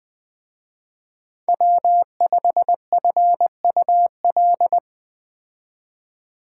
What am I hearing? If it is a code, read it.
Morse code, heard as a keyed tone, about 20 wpm, 700 Hz. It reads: W5FUL